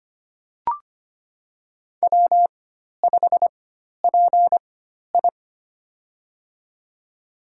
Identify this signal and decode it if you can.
Morse code, heard as a keyed tone, about 25 wpm, spaced as for 12 wpm, 700 Hz. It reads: W5PI
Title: USA Call Signs - 4 Characters - 4x Character Spacing - 25wpm